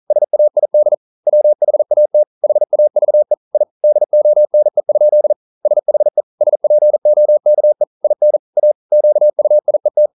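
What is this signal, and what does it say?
Morse code, heard as a keyed tone, about 42 wpm, 600 Hz. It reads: SAID WHAT HAVE I DONE? SHE SPOKE IN A QUIET
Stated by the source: Morse code practice at 41 wpm, from CW med Gnister.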